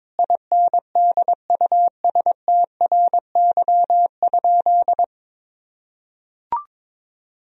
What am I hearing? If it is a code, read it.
Morse code, heard as a keyed tone, about 22 wpm, 700 Hz. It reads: INDUSTRY?
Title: Sentences from Top 1000 Words - Review Part 4 - 22wpm